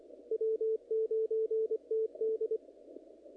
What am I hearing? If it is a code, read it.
Morse code, heard as a keyed tone, about 25 wpm, 440 Hz, with radio noise. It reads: W9TD
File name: contests / naqp-cw-k5tr-op-k5pi- -1900